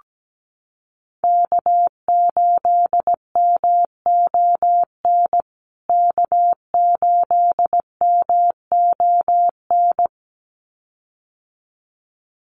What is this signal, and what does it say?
Morse code, heard as a keyed tone, about 17 wpm, 700 Hz. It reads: K8MON K8MON
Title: USA Call Signs Repeated 2x 17wpm